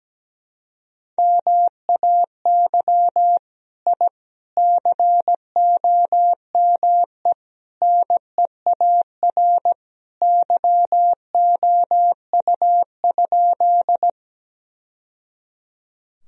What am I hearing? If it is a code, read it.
Morse code, heard as a keyed tone, about 17 wpm, 700 Hz. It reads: MAY I COME NEAR YOU?
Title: Sentences from Top 1000 Words - Review Part 1 - 17wpm